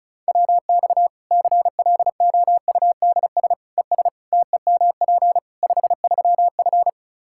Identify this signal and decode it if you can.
Morse code, heard as a keyed tone, about 36 wpm, 700 Hz. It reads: WX CLOUDS ES TEMP 53F
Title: Practice QSO - Exchange 2 - Info - 35wpm